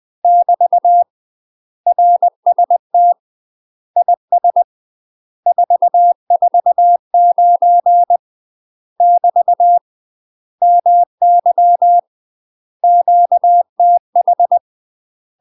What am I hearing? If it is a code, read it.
Morse code, heard as a keyed tone, about 20 wpm, 700 Hz. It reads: <BT> RST IS 449 <BT> MY QTH
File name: UK MASTER QSOs TRAINING / 20WPM / UK MASTER QSOs 0025